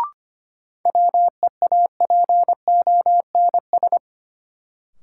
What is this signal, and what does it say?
Morse code, heard as a keyed tone, about 25 wpm, 700 Hz. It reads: WEAPONS
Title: Sets of 5 Words Encore - New Word Drill - 25wpm